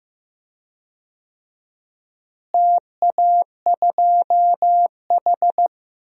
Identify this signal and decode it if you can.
Morse code, heard as a keyed tone, about 15 wpm, 700 Hz. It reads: TA2H
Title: DXCC-100 Call Signs - 4 Characters - Repeated 2x - 15wpm